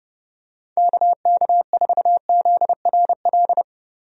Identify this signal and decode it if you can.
Morse code, heard as a keyed tone, about 32 wpm, 700 Hz. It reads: KK4ZRL
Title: USA Call Signs - 6 Characters - Repeated 2x - 30wpm